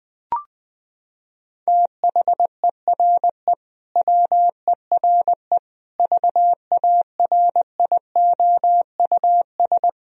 Morse code, heard as a keyed tone, about 20 wpm, 700 Hz. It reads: THERE WERE VARIOUS
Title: Sentences from Top 1000 Words - Review Part 4 - 20wpm